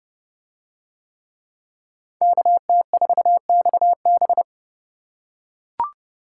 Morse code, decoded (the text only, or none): KT4XB